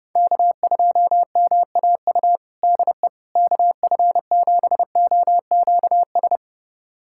Morse code, heard as a keyed tone, about 32 wpm, 700 Hz. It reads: K2MAU DE KF7OQS
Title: USA Callsign DE Callsign 30wpm